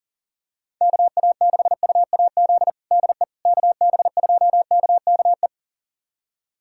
Morse code, heard as a keyed tone, about 40 wpm, 700 Hz. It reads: KA6UAZ DE KB2KKE
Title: USA Callsign DE Callsign - 6 Characters - 40wpm